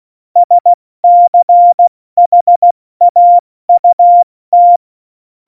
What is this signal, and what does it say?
Morse code, heard as a keyed tone, about 16 wpm, 700 Hz. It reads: SCHAUT
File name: Morsekurs / Witze / Witz 29. Januar 2019, 70 F80